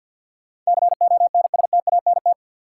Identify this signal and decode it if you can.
Morse code, heard as a keyed tone, about 52 wpm, 700 Hz. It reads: CONSTANT
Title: Sentences from Top 1500 Words - New Word Drill Alt - 50wpm